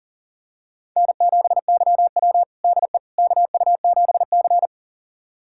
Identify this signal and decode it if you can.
Morse code, heard as a keyed tone, about 40 wpm, 700 Hz. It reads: N7YW DE KU7C
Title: USA Callsign DE Callsign - 4 Characters - 40wpm